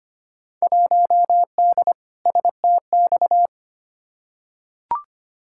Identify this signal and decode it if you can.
Morse code, heard as a keyed tone, about 25 wpm, 700 Hz. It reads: 1D STX
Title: Field Day Exchanges - 25wpm